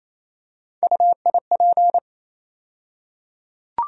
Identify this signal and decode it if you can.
Morse code, heard as a keyed tone, about 28 wpm, 700 Hz. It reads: UIP
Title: USA Call Sign Postfixes - 3 Letters 28wpm